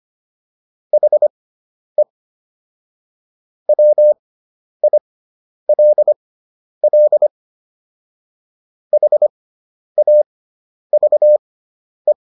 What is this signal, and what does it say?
Morse code, heard as a keyed tone, about 25 wpm, spaced as for 10 wpm, 600 Hz. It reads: HE WILL HAVE